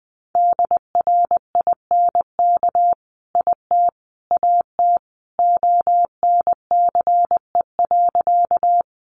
Morse code, heard as a keyed tone, about 20 wpm, 700 Hz. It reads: DRINK IT AT ONCE.
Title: Sentences from Top 1500 Words 20wpm